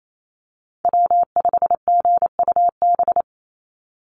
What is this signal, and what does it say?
Morse code, heard as a keyed tone, about 28 wpm, 700 Hz. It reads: W5GUB
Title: USA Call Signs 28wpm